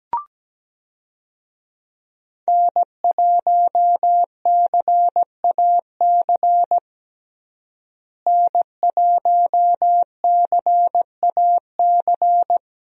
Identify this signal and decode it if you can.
Morse code, heard as a keyed tone, about 17 wpm, 700 Hz. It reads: N1CAC N1CAC
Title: USA Call Signs - 5 Characters - Repeated 2x - 17wpm